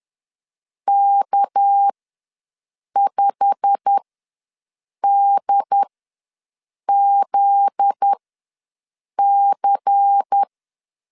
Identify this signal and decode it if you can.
Morse code, heard as a keyed tone, about 10 wpm, spaced as for 6 wpm, 800 Hz. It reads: K5DZC
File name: LEZIONI / 8 YJ45 / 8 YJ45 ws 10wpm 3843